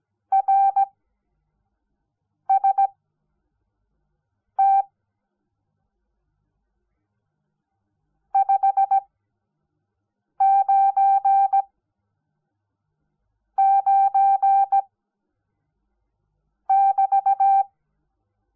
Morse code, heard as a keyed tone, about 17 wpm, spaced as for 5 wpm, 780 Hz. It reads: RST 599 <BT>
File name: Onlinelehrgang / morsen / DJ4UF Morsen05